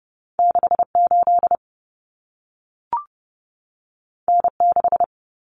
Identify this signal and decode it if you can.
Morse code, heard as a keyed tone, about 30 wpm, 700 Hz. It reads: 68 N6
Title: Common Cut Numbers - 2-Digit 30wpm